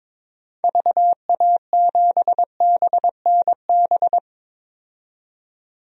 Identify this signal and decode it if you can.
Morse code, heard as a keyed tone, about 22 wpm, 700 Hz. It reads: VA7BNB
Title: DXCC-100 Call Signs - 6 Characters - 22wpm